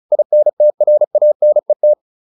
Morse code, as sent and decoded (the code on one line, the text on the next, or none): .. -. - .-. .- -. -
INTRANET